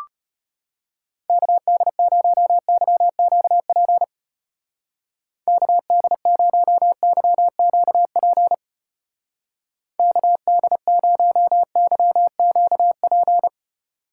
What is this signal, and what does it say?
Morse code, heard as a keyed tone, about 31 wpm, 700 Hz. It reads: KD0YQP KD0YQP KD0YQP
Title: Speed Racing - USA 2x3 Call Signs 25wpm